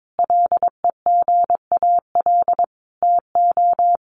Morse code, heard as a keyed tone, about 22 wpm, 700 Hz. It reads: LEGAL TO
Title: Sentences from Top 1200 Words 22wpm